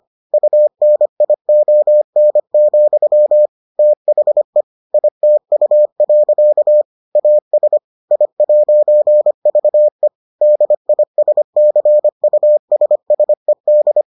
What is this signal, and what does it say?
Morse code, heard as a keyed tone, about 25 wpm, 600 Hz. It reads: UNION, THE ITU. AS I'VE DISCUSSED